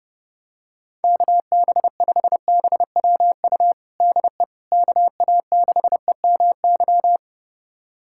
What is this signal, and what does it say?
Morse code, heard as a keyed tone, about 30 wpm, 700 Hz. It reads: KB5BWU DE KA6EMY
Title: USA Callsign DE Callsign - 6 Characters - 30wpm